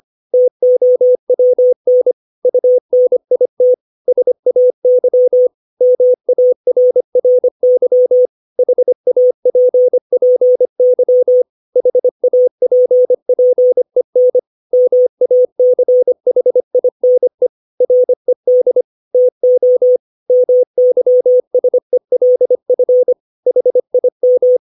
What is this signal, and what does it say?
Morse code, heard as a keyed tone, about 25 wpm, 500 Hz. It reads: TOWN UNIT SAY MARRY HAPPY HAPPEN MACHINE RED TO MYSELF HIM
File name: 632 words / 632 Words 25WPM